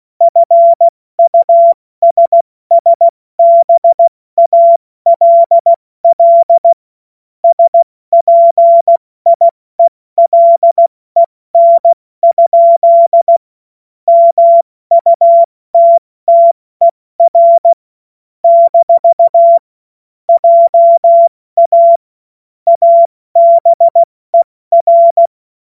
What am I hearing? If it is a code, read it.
Morse code, heard as a keyed tone, about 16 wpm, spaced as for 14 wpm, 680 Hz. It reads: FUSSBALL SPIELEN? MUTTER - JA ABER
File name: Morsekurs / Witze / Witz 07. September 2021, 70 F80